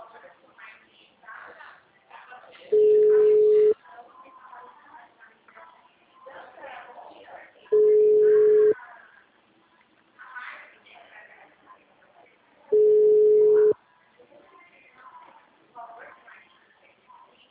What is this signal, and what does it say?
Ringback tone (Europe, 425 Hz), with speech.